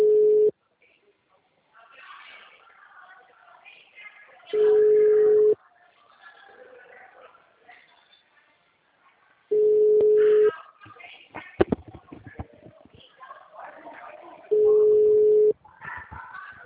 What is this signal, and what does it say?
Ringback tone (Europe, 425 Hz).